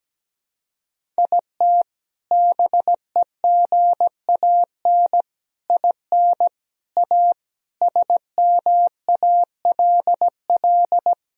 Morse code, heard as a keyed tone, about 17 wpm, 700 Hz. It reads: IT BEGAN IN A SMALL
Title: Sentences from Top 1000 Words - Review Part 1 - 17wpm